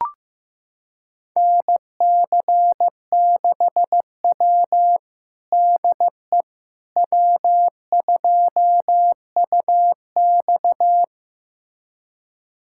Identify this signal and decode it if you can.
Morse code, heard as a keyed tone, about 15 wpm, 700 Hz. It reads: NC6W DE W2UX
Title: USA Callsign DE Callsign - 4 Characters - 15wpm